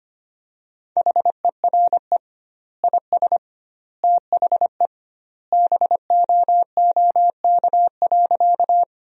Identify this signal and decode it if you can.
Morse code, heard as a keyed tone, about 25 wpm, 700 Hz. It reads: HERE IS THE BOOK.
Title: 2x Word-Spacing - Sentences from Top 300 Words - 25wpm